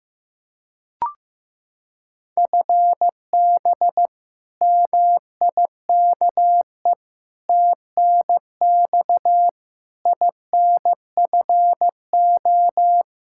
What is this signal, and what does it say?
Morse code, heard as a keyed tone, about 15 wpm, 700 Hz. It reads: FB MIKE TNX INFO
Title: Practice QSO - Exchange 3 - More Info - 15wpm